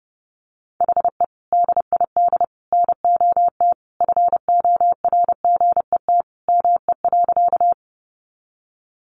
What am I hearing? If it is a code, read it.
Morse code, heard as a keyed tone, about 30 wpm, 700 Hz. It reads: HE DID NOT FORGET ME.
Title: Sentences from Top 1200 Words 30wpm